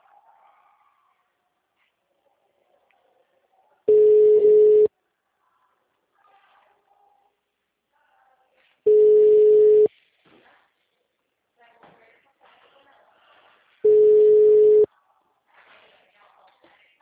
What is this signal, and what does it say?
Ringback tone (Europe, 425 Hz), with speech.